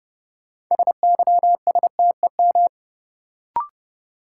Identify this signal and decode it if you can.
Morse code, heard as a keyed tone, about 29 wpm, 700 Hz. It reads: SYSTEM
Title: Sentences from Top 1000 Words - Review Part 3 - New Word Drill - 30wpm